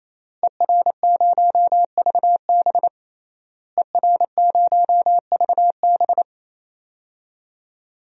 Morse code, heard as a keyed tone, about 28 wpm, 700 Hz. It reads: ER0VB ER0VB
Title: DXCC-100 Call Signs - 5 Characters - Repeated 3x - 28wpm